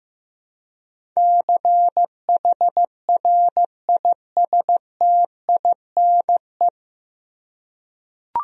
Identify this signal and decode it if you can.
Morse code, heard as a keyed tone, about 15 wpm, 700 Hz. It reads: CHRISTINE
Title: USA Common Names - 15wpm